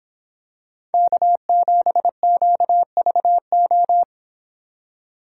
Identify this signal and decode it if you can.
Morse code, heard as a keyed tone, about 26 wpm, 700 Hz. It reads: K7QVO